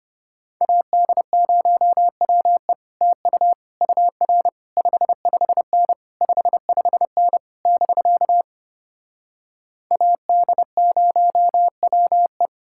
Morse code, heard as a keyed tone, about 33 wpm, 700 Hz. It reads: AD0WE TU UR 55N 55N <BK> AD0WE